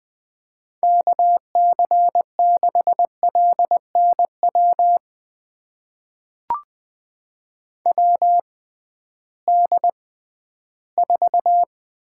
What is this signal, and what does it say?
Morse code, heard as a keyed tone, about 20 wpm, spaced as for 13 wpm, 700 Hz. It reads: KC6LNW WD4
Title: USA Call Signs - 6 Characters - 6x Character Spacing - 20wpm